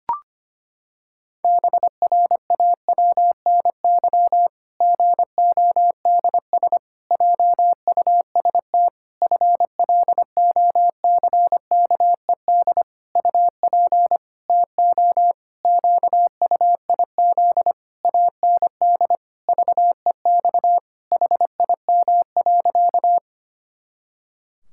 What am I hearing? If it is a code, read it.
Morse code, heard as a keyed tone, about 25 wpm, 700 Hz. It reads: BRAWNY GODS JUST FLOCKED UP TO QUIZ AND VEX HIM.
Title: Pangram Challenge v2 - 25wpm